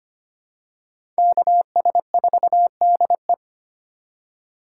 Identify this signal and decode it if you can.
Morse code, heard as a keyed tone, about 25 wpm, 700 Hz. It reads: KS4DE
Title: USA Call Signs - 4x Character Spacing - 25wpm